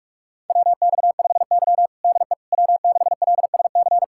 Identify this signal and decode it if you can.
Morse code, heard as a keyed tone, about 45 wpm, 700 Hz. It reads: WX5Y DE W6LSC
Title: USA Callsign DE Callsign 45wpm